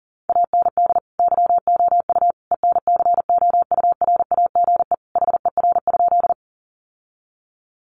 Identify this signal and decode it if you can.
Morse code, heard as a keyed tone, about 40 wpm, 700 Hz. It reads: AND YOU ENCOURAGE HER?
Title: Sentences from Top 1400 Words 40wpm